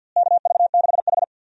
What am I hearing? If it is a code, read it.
Morse code, heard as a keyed tone, about 53 wpm, 700 Hz. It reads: KV6H
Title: USA Call Signs - 4 Characters - Repeated 3x - 50wpm